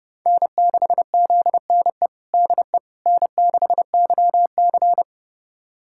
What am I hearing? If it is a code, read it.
Morse code, heard as a keyed tone, about 29 wpm, 700 Hz. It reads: N6ZNE DE N6YC
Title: USA Callsign DE Callsign 30wpm